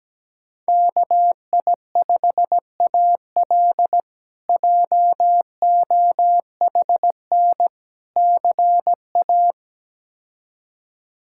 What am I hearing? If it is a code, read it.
Morse code, heard as a keyed tone, about 17 wpm, 700 Hz. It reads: KI5AL JOHN CA
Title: K1USN SST QSO Practice - 17wpm